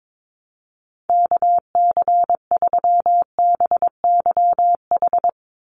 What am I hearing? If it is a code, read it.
Morse code, heard as a keyed tone, about 22 wpm, 700 Hz. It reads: KC3BYH
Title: USA Call Signs 22wpm